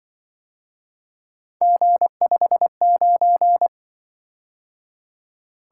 Morse code, heard as a keyed tone, about 24 wpm, 700 Hz. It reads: G59
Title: Speed-Racing - Three Letter-Number Combinations - 20wpm